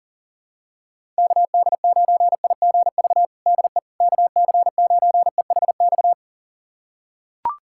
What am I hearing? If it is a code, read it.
Morse code, heard as a keyed tone, about 40 wpm, 690 Hz. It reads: KD9IGV DE KC9EHX
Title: USA Callsign DE Callsign - 6 Characters - 40wpm